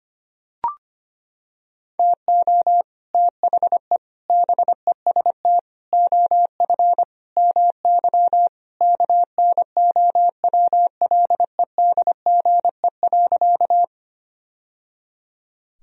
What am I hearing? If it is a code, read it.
Morse code, heard as a keyed tone, about 25 wpm, 700 Hz. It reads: TO THE BEST OF MY KNOWLEDGE.
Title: Sentences from Top 1000 Words - Review Part 3 - 25wpm